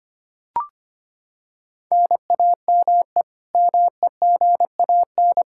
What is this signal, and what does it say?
Morse code, heard as a keyed tone, about 25 wpm, 700 Hz. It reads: NAME MEGAN
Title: Practice QSO - Exchange 1 - Beginning - 25wpm